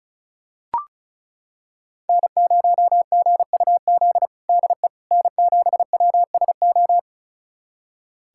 Morse code, heard as a keyed tone, about 36 wpm, 700 Hz. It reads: N0GUZ DE N7WSO
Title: USA Callsign DE Callsign - 5 Characters - 35wpm